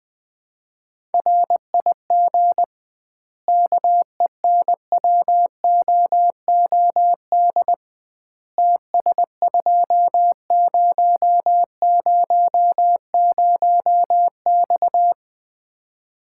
Morse code, heard as a keyed tone, about 20 wpm, 700 Hz. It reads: RIG KENWOOD TS2000X
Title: Common QSO Phrases - 2x Word-Spacing - 20wpm